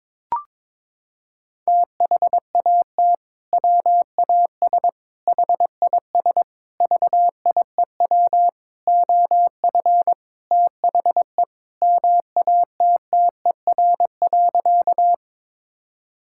Morse code, heard as a keyed tone, about 22 wpm, 700 Hz. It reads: THAT WAS HIS VIEW OF THE MATTER.
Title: Sentences from Top 1000 Words - Review Part 6 - 22wpm